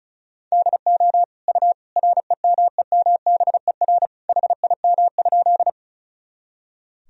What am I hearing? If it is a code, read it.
Morse code, heard as a keyed tone, about 36 wpm, 700 Hz. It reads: DO U REMEMBER HIM?
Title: Sentences from Top 400 Words and Abbreviations 35wpm